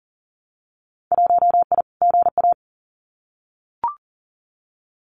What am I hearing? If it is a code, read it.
Morse code, heard as a keyed tone, about 40 wpm, 700 Hz. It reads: 1I GA
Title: Winter Field Day Exchanges Encore 40wpm